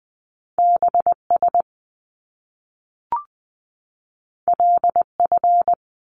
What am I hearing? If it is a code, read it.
Morse code, heard as a keyed tone, about 20 wpm, 700 Hz. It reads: BS LF
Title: Top Two Letter Combinations in Words 20wpm